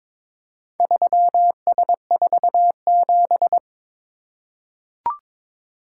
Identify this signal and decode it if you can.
Morse code, heard as a keyed tone, about 22 wpm, 700 Hz. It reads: 3S47
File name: practice / 22wpm / Four Letter-Number Combinations - 22wpm